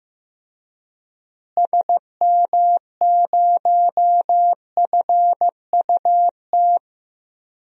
Morse code, heard as a keyed tone, about 15 wpm, 700 Hz. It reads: SM0FUT
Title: DXCC-100 Call Signs - 6 Characters - Repeated 2x - 15wpm